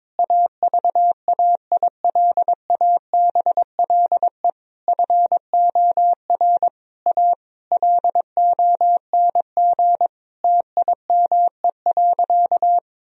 Morse code, heard as a keyed tone, about 22 wpm, 700 Hz. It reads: AVAILABLE FOR A LONG TIME.
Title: Sentences from Top 1000 Words - Review Part 4 - 22wpm